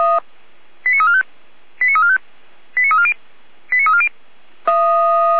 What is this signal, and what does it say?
Five-tone selective call (ZVEI), heard as quick runs of five tones, tones of about 70 ms. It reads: 89345 89345 89340 89340
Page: ZVEI